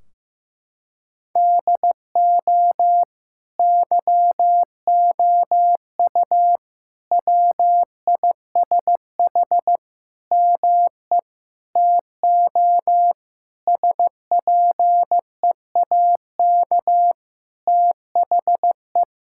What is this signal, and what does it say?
Morse code, heard as a keyed tone, about 15 wpm, 700 Hz. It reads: DO YOU WISH ME TO SPEAK THE
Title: Sentences from Top 1600 Words 15wpm